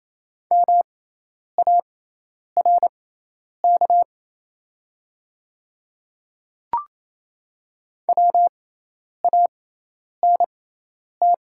Morse code, heard as a keyed tone, about 27 wpm, spaced as for 10 wpm, 690 Hz. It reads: MARK WANT
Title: 6x Character Spacing - Top 300 Words - 28wpm